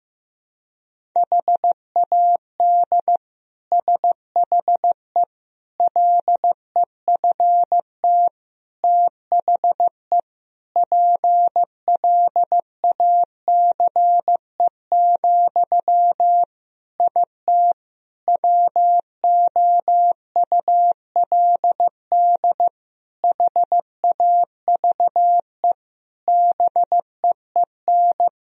Morse code, heard as a keyed tone, about 15 wpm, 700 Hz. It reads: HAD SHE LEFT THE PLACE, IT WOULD HAVE BEEN
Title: Sentences from Top 1000 Words - Review Part 3 - 15wpm